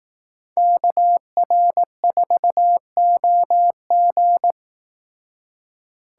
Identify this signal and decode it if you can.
Morse code, heard as a keyed tone, about 18 wpm, 700 Hz. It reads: KR4OG
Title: Speed Racing - USA Call Signs 15wpm